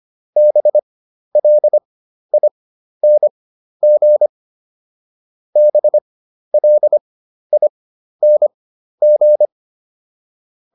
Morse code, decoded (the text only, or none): BLING BLING